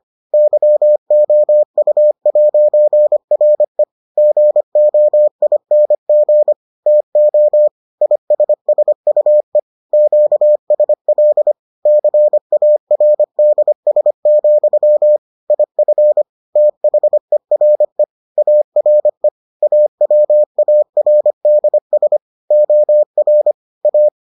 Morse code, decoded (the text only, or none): YOU'RE GOING TO ISSUE QSL CARDS, IF THERE ARE AWARDS OR A